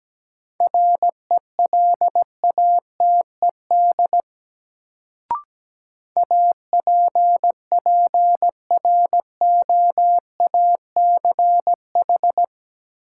Morse code, heard as a keyed tone, about 17 wpm, 700 Hz. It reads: RELATED APPROACH
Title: Sentences from Top 1400 Words - New Word Drill Alt - 17wpm